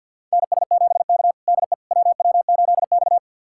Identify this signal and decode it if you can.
Morse code, heard as a keyed tone, about 52 wpm, 700 Hz. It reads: NS7K DE WW8X
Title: USA Callsign DE Callsign - 4 Characters - 50wpm